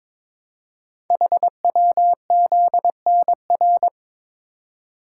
Morse code, heard as a keyed tone, about 22 wpm, 700 Hz. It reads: HWZNR
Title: Code Groups - 5 Letter:Numbers - 22wpm